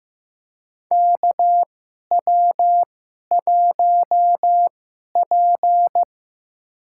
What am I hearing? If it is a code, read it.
Morse code, heard as a keyed tone, about 15 wpm, spaced as for 11 wpm, 700 Hz. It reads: KW1P
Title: USA Call Signs - 4 Characters - 2x Character Spacing - 15wpm